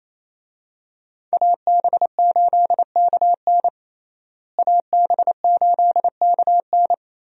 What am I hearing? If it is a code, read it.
Morse code, heard as a keyed tone, about 27 wpm, 700 Hz. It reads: AB8KN AB8KN
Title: USA Call Signs - 5 Characters - Repeated 2x - 28wpm